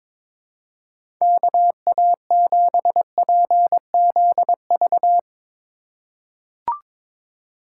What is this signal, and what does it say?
Morse code, heard as a keyed tone, about 22 wpm, 700 Hz. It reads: KA7PZV
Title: USA Call Signs - 6 Characters - 4x Character Spacing - 22wpm